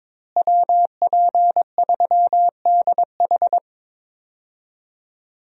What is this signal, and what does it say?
Morse code, heard as a keyed tone, about 22 wpm, 700 Hz. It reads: WP3DH
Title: USA Call Signs - 5 Characters - Repeated 2x - 22wpm